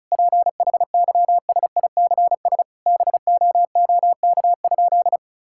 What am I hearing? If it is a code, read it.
Morse code, heard as a keyed tone, about 36 wpm, 700 Hz. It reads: PHYSICS BOOK?
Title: Sentences from Top 1000 Words - Review Part 5 - 35wpm